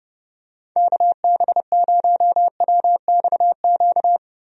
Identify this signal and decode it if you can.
Morse code, heard as a keyed tone, about 29 wpm, 700 Hz. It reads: KB0WXQ
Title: USA Call Signs - 6 Characters - Repeated 2x - 30wpm